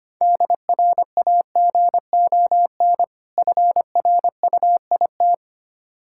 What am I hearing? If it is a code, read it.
Morse code, heard as a keyed tone, about 25 wpm, 690 Hz. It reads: DRAGON FRUIT
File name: practice / 25wpm / Difficult Words with Hints - 25wpm